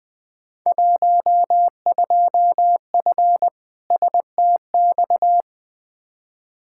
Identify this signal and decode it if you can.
Morse code, heard as a keyed tone, about 21 wpm, 700 Hz. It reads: 12F STX
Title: Field Day Exchanges Encore - 20wpm